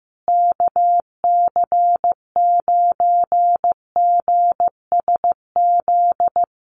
Morse code, heard as a keyed tone, about 15 wpm, 700 Hz. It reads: KC9GSZ
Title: USA Call Signs Repeated 3x 15wpm